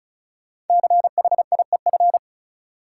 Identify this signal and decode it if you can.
Morse code, heard as a keyed tone, about 36 wpm, 700 Hz. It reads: CHIEF
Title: Sets of 5 Words Encore - New Word Drill - 35wpm